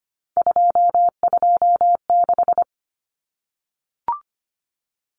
Morse code, heard as a keyed tone, about 25 wpm, 700 Hz. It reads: 226
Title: Numbers - 3-Digit 25wpm